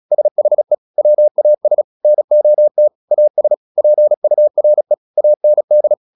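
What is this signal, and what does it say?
Morse code, heard as a keyed tone, about 36 wpm, 600 Hz. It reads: SHE WAS NOT AS PURE AND
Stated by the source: Morse practice at 36 wpm, text of Ken Follett's novel Never, from CW med Gnister.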